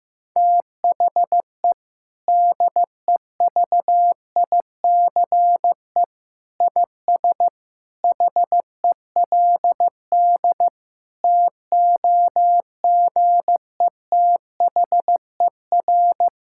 Morse code, heard as a keyed tone, about 15 wpm, 700 Hz. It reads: THE DEVICE IS HELD TOGETHER
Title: Sentences from Top 1600 Words 15wpm